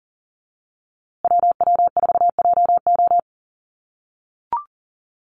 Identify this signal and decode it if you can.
Morse code, heard as a keyed tone, about 40 wpm, 700 Hz. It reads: WW4JO
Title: USA Call Signs Repeated 2x 40wpm